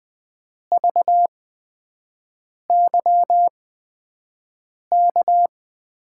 Morse code, heard as a keyed tone, about 20 wpm, spaced as for 5 wpm, 700 Hz. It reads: VYK
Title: USA Call Signs - 8x Character Spacing - 20wpm